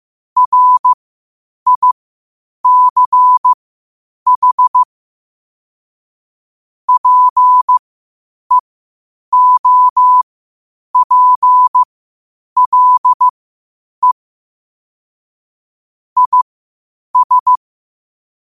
Morse code, heard as a keyed tone, about 15 wpm, spaced as for 8 wpm, 1000 Hz. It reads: RICH PEOPLE IS
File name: qotdfiles / 8wpm